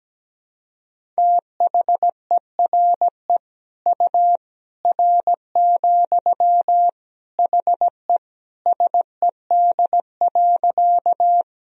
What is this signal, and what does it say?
Morse code, heard as a keyed tone, about 17 wpm, 700 Hz. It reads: THERE U R, HE SED.
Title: Sentences from Top 100 Words and Abbreviations 17wpm